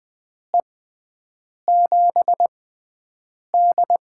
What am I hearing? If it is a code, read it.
Morse code, heard as a keyed tone, about 20 wpm, spaced as for 7 wpm, 700 Hz. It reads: E7D
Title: USA Call Signs - 6 Characters - 6x Character Spacing - 20wpm